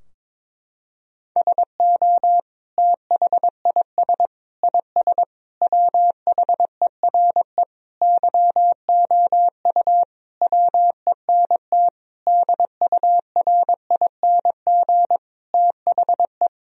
Morse code, heard as a keyed tone, about 22 wpm, 700 Hz. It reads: SO THIS IS WHERE YOU WENT DURING THE
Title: Sentences from Top 1000 Words - Review Part 2 - 22wpm